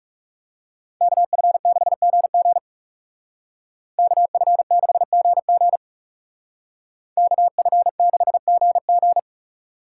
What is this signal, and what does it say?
Morse code, heard as a keyed tone, about 40 wpm, 700 Hz. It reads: KF6GG KF6GG KF6GG